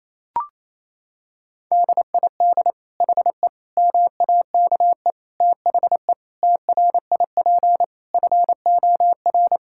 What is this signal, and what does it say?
Morse code, heard as a keyed tone, about 28 wpm, 700 Hz. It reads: DID HE MAKE THE TRIP FOR
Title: Sentences from Top 1600 Words 28wpm